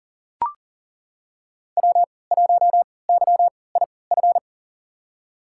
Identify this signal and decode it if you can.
Morse code, heard as a keyed tone, about 38 wpm, spaced as for 22 wpm, 700 Hz. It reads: W1YIF
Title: USA Call Signs - 5 Characters - 3x Character Spacing - 40wpm